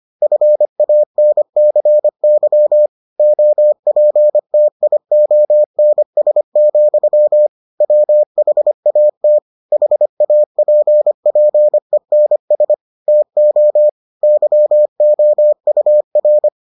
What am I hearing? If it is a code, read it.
Morse code, heard as a keyed tone, about 25 wpm, 600 Hz. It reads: FANCY OPTIONS, WHAT HAPPENS TO YOUR